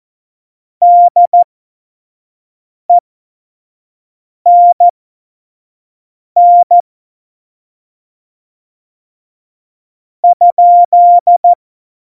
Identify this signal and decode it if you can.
Morse code, heard as a keyed tone, about 14 wpm, spaced as for 5 wpm, 700 Hz. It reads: DENN ?
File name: Morsekurs / Witze / Witz 28. August 2018, 25 F70